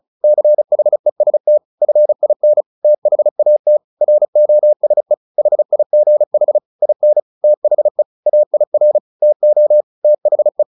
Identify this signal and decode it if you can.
Morse code, heard as a keyed tone, about 34 wpm, 600 Hz. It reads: CHEST FIN THAT ROSE HIGH IN THE AIR TO THE
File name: the old man and the sea 2 35wpm